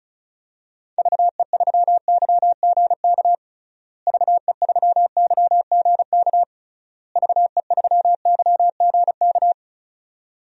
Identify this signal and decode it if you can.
Morse code, heard as a keyed tone, about 34 wpm, 700 Hz. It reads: VE3YGK VE3YGK VE3YGK
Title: DXCC-100 Call Signs - 6 Characters - Repeated 3x - 35wpm